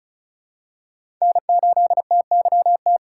Morse code, heard as a keyed tone, about 36 wpm, 700 Hz. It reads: N8TYT
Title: USA Call Signs - 5 Characters - 3x Character Spacing - 35wpm